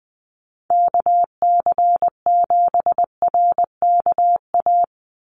Morse code, heard as a keyed tone, about 20 wpm, 700 Hz. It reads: KC7RKA
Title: USA Call Signs 20wpm